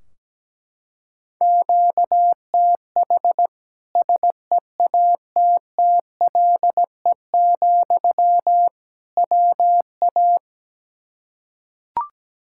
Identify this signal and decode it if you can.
Morse code, heard as a keyed tone, about 17 wpm, 700 Hz. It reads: QTH SEATTLE, WA
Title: Practice QSO - Exchange 1 - Beginning - 17wpm